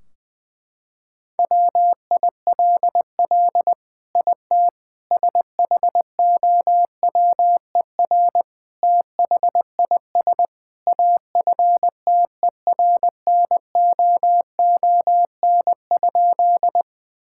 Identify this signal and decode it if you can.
Morse code, heard as a keyed tone, about 21 wpm, 700 Hz. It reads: WILL IT SHOWER THIS AFTERNOON?
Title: Sentences from Top 1600 Words 20wpm